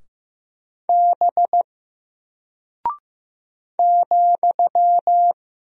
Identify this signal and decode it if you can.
Morse code, heard as a keyed tone, about 15 wpm, 700 Hz. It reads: B ,
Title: Single Character (Letter:Number:Punctuation:Prosign) 15wpm